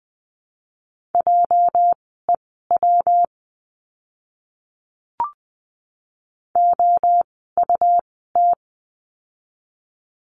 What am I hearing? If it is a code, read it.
Morse code, heard as a keyed tone, about 20 wpm, spaced as for 14 wpm, 700 Hz. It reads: JEW OUT
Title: 206 Lesson 33f - Letter j - Words 20wpm